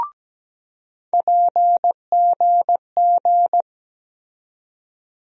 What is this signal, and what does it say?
Morse code, heard as a keyed tone, about 17 wpm, 700 Hz. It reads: PGG
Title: USA Call Sign Postfixes - 3 Letters 17wpm